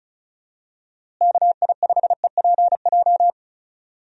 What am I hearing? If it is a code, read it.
Morse code, heard as a keyed tone, about 34 wpm, 700 Hz. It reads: KI5EPJ
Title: USA Call Signs - 6 Characters - Repeated 2x - 35wpm